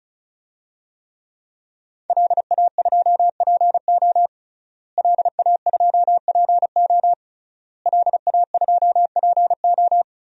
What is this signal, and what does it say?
Morse code, heard as a keyed tone, about 36 wpm, 700 Hz. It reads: LA2PO LA2PO LA2PO
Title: DXCC-100 Call Signs - 5 Characters - Repeated 3x - 35wpm